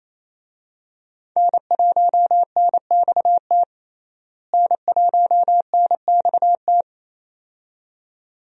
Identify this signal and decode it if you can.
Morse code, heard as a keyed tone, about 27 wpm, 700 Hz. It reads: N1NXT N1NXT
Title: USA Call Signs - 5 Characters - Repeated 2x - 28wpm